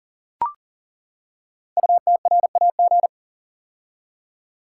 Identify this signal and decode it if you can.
Morse code, heard as a keyed tone, about 39 wpm, 700 Hz. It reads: UTRAG
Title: Code Groups - 5 Letter:Numbers - 40wpm